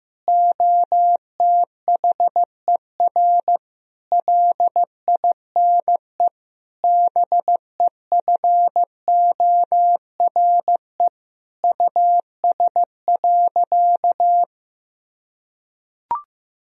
Morse code, heard as a keyed tone, about 15 wpm, 700 Hz. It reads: OTHER LINE BEFORE US.
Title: Sentences from Top 1000 Words - Review Part 1 - 15wpm